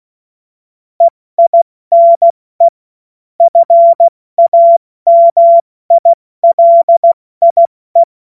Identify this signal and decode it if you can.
Morse code, heard as a keyed tone, about 16 wpm, 680 Hz. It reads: EINE FAMILIE
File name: Morsekurs / Witze / Witz 16. November 2021, 70 F80